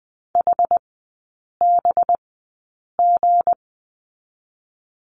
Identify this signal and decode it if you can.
Morse code, heard as a keyed tone, about 20 wpm, spaced as for 8 wpm, 700 Hz. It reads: HBG
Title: 184 Lesson 30e - Letter b - Three Characters 20wpm